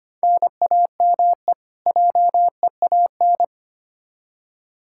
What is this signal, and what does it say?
Morse code, heard as a keyed tone, about 25 wpm, 700 Hz. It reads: NAME JEAN
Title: Practice QSO - Exchange 1 - Beginning - 25wpm